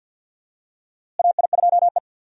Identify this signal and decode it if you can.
Morse code, heard as a keyed tone, about 51 wpm, 700 Hz. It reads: AI2E
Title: USA Call Signs - 4 Characters - Repeated 3x - 50wpm